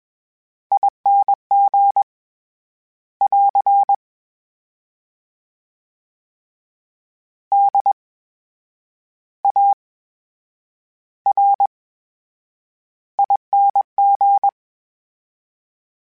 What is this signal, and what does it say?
Morse code, heard as a keyed tone, about 21 wpm, spaced as for 9 wpm, 800 Hz. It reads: ING <AR> DARING
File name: Special INGs-mp3s / Special INGs-mp3s / VerySlowToFaster Ing @20